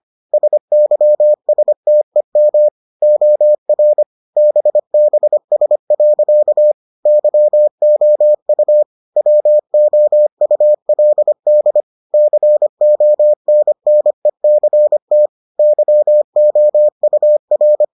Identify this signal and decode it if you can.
Morse code, heard as a keyed tone, about 25 wpm, 600 Hz. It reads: SYSTEM OR BBS. YOU WOULD CONNECT YOUR